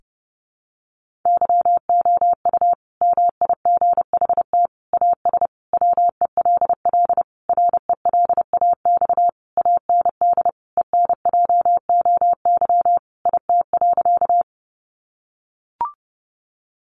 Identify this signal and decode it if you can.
Morse code, heard as a keyed tone, about 30 wpm, 700 Hz. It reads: YOU MIGHT AS WELL RELAX AND ENJOY IT.
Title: Sentences from Top 1400 Words 30wpm